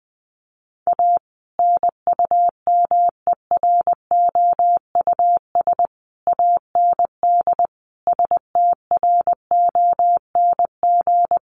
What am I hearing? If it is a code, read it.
Morse code, heard as a keyed tone, about 19 wpm, 700 Hz. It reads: A NUMEROUS AND STRONG
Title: Sentences from Top 1400 Words 20wpm